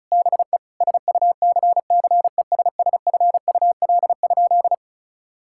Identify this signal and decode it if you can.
Morse code, heard as a keyed tone, about 34 wpm, 700 Hz. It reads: BE SUCCESSFUL?
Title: Sentences from Top 1000 Words - Review Part 5 - 35wpm